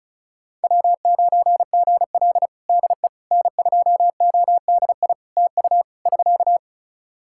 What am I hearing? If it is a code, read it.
Morse code, heard as a keyed tone, about 34 wpm, 700 Hz. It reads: W9GL DE N2ODI TU <SK>
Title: Practice QSO - Exchange 4 - Ending - 35wpm